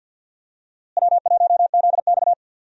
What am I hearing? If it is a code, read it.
Morse code, heard as a keyed tone, about 47 wpm, 700 Hz. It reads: W1ZX
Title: USA Call Signs - 4 Characters - Repeated 2x - 50wpm